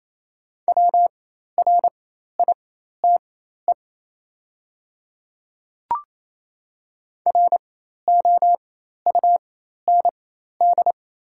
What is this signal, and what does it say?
Morse code, heard as a keyed tone, about 27 wpm, spaced as for 13 wpm, 690 Hz. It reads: WRITE ROUND
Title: 4x Character Spacing - Top 300 Words - 28wpm